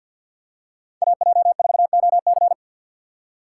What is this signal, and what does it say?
Morse code, heard as a keyed tone, about 53 wpm, 700 Hz. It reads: AJ4OC
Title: USA Call Signs - 5 Characters - 6x Character Spacing - 50wpm